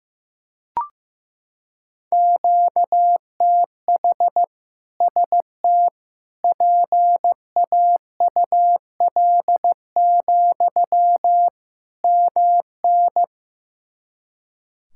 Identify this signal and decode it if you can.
Morse code, heard as a keyed tone, about 15 wpm, 700 Hz. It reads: QTH ST PAUL, MN
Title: K1USN SST QSO Practice 15wpm